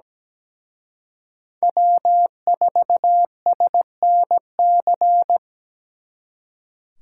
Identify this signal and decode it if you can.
Morse code, heard as a keyed tone, about 20 wpm, 700 Hz. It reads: W4SNC
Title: Speed Racing - USA 1x3 Call Signs 17wpm